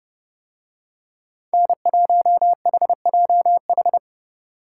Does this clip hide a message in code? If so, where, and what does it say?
Yes: Morse code from 1.5 to 4.0 s, heard as a keyed tone, about 30 wpm, 700 Hz. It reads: N1HJH